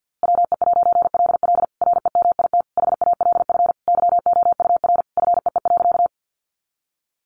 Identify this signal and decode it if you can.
Morse code, heard as a keyed tone, about 50 wpm, 700 Hz. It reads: WE'LL REMIT HALF YOUR FEE.
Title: A Sound Of Thunder 50wpm